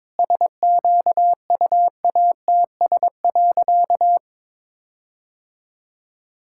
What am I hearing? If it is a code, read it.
Morse code, heard as a keyed tone, about 22 wpm, 700 Hz. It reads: SQUATS.
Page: morse-code-ninja-practice-22wpm